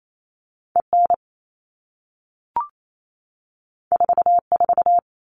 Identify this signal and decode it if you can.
Morse code, heard as a keyed tone, about 28 wpm, 700 Hz. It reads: EN 44
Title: All Cut Numbers - 2-Digit 28wpm